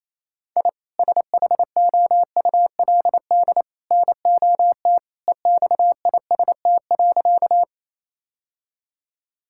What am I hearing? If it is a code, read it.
Morse code, heard as a keyed tone, about 28 wpm, 700 Hz. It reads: I SHOULD NOT EXIST.
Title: Sentences from Top 1000 Words - Review Part 5 - 28wpm